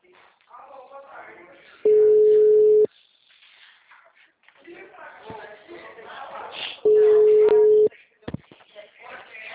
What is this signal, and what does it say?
Ringback tone (Europe, 425 Hz), with speech.